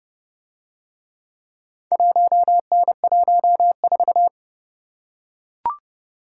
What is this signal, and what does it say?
Morse code, heard as a keyed tone, about 29 wpm, 700 Hz. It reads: 1N14